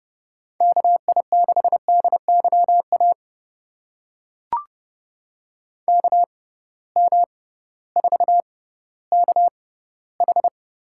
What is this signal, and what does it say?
Morse code, heard as a keyed tone, about 30 wpm, spaced as for 16 wpm, 700 Hz. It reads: KI6DYA KM4KH
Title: USA Call Signs - 6 Characters - 6x Character Spacing - 30wpm